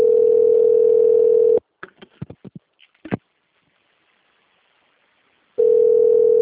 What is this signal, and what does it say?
Ringback tone (North America), 440 + 480 Hz.
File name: rg-1600-unknown- - - .4573